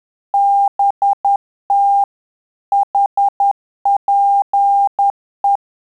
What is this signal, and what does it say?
Morse code, heard as a keyed tone, about 10 wpm, 800 Hz. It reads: BT HPE
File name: Telegraphy-group / QSO CW 50 CPM